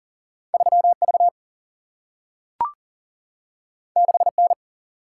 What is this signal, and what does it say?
Morse code, heard as a keyed tone, about 40 wpm, 700 Hz. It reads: 3 V 6 N
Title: Single-Letter Single-Number 40wpm